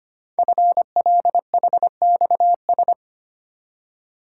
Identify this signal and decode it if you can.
Morse code, heard as a keyed tone, about 25 wpm, 700 Hz. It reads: FLHXS
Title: Code Groups - 5 Letters - 25wpm